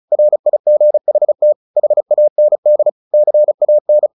Morse code, heard as a keyed tone, about 34 wpm, 600 Hz. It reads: RIGHT HAND CAN
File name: the old man and the sea 2 35wpm